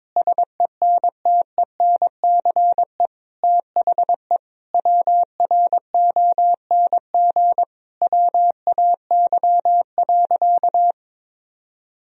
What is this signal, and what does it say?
Morse code, heard as a keyed tone, about 22 wpm, 700 Hz. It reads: SENTENCE THE WRONG WAY.
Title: Sentences from Top 1000 Words - Review Part 1 - 22wpm